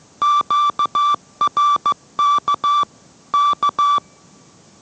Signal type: Morse code